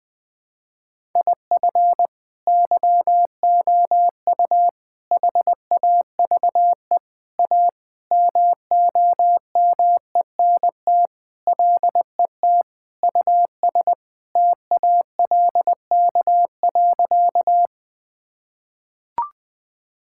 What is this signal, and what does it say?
Morse code, heard as a keyed tone, about 21 wpm, 700 Hz. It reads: IF YOU HAVE A MOMENT LET US TALK.
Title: Sentences with 4-Word Phrases - Spoken Beginning - 20wpm